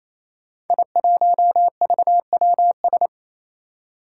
Morse code, heard as a keyed tone, about 27 wpm, 700 Hz. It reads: I1VWS
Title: Five Challenging Letter-Number Combinations 28wpm